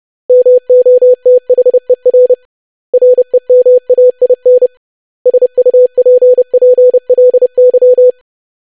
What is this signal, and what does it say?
Morse code, heard as a keyed tone, about 30 wpm, 500 Hz. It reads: MOTHER REMAIN SUPPLY
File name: lesson / web / basic500-25wpm